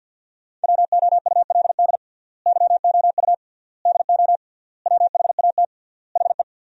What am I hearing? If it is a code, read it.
Morse code, heard as a keyed tone, about 50 wpm, 700 Hz. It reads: WOULD YOU DO WHAT HE
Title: 3x Word-Spacing - Sentences from Top 100 Words - 50wpm